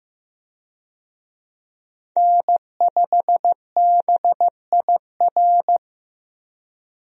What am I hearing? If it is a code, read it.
Morse code, heard as a keyed tone, about 16 wpm, 700 Hz. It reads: N5BIR
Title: USA Call Signs - 5 Characters - Repeated 3x - 15wpm